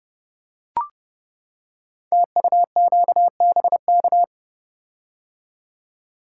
Morse code, heard as a keyed tone, about 30 wpm, 700 Hz. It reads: TUQBK
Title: Code Groups - 5 Letters - 30wpm